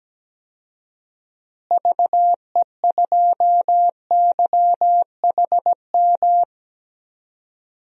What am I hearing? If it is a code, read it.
Morse code, heard as a keyed tone, about 17 wpm, 700 Hz. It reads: VE2YHM
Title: DXCC-100 Call Signs - 6 Characters - Repeated 2x - 17wpm